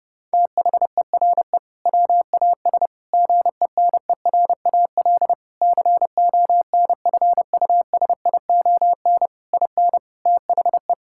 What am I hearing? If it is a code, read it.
Morse code, heard as a keyed tone, about 30 wpm, 700 Hz. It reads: THERE WAS GENERAL CONFUSION IN THE
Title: Sentences from Top 1600 Words 30wpm